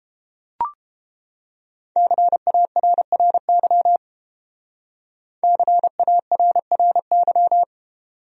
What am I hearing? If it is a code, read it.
Morse code, heard as a keyed tone, about 28 wpm, 700 Hz. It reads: CARRY CARRY